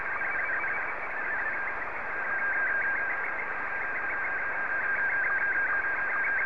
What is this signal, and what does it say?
SSTV (slow-scan television), Scottie 1.